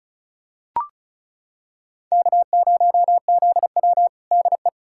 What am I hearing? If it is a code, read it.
Morse code, heard as a keyed tone, about 34 wpm, 700 Hz. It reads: K0ZW DE
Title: USA Callsign DE Callsign - 4 Characters - 35wpm